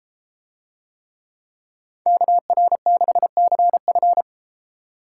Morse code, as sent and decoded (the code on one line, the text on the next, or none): -.- .-. -.... -.-. ..-.
KR6CF